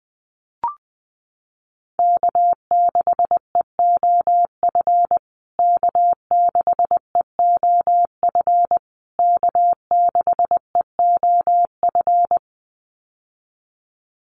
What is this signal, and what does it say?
Morse code, heard as a keyed tone, about 20 wpm, 700 Hz. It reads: K6EOF K6EOF K6EOF
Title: USA Call Signs Repeated 3x 20wpm